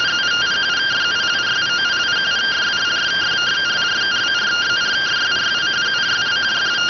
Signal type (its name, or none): two-tone teleprinter signal